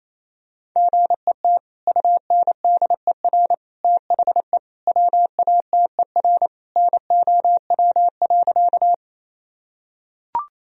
Morse code, heard as a keyed tone, about 28 wpm, 700 Hz. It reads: GET UNDER THE WATER NOW.
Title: Sentences from Top 1000 Words - Review Part 1 - 28wpm